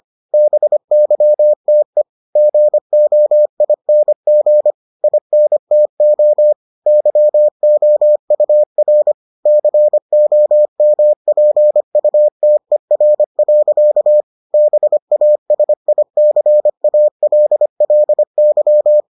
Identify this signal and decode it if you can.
Morse code, heard as a keyed tone, about 25 wpm, 600 Hz. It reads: BYTE GOING INTO YOUR COMPUTER. BASICALLY